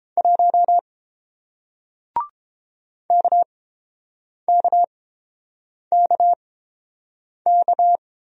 Morse code, heard as a keyed tone, about 30 wpm, spaced as for 7 wpm, 700 Hz. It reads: 1 KKKK